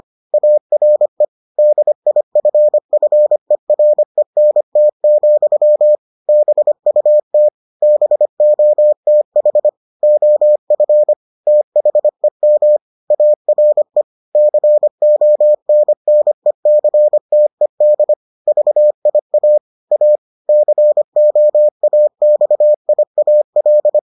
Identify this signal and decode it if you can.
Morse code, heard as a keyed tone, about 25 wpm, 600 Hz. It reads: ARE DIFFERENT, BUT BOTH OF THEM ARE CONNECTED VIA A COAXIAL